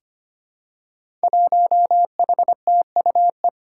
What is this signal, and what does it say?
Morse code, heard as a keyed tone, about 25 wpm, 700 Hz. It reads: 1HTUE
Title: Five Challenging Letter-Number Combinations 25wpm